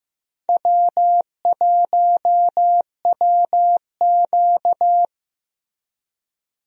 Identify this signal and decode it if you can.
Morse code, heard as a keyed tone, about 15 wpm, 700 Hz. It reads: W1WQ
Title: USA Call Signs - 4 Characters - Repeated 2x - 15wpm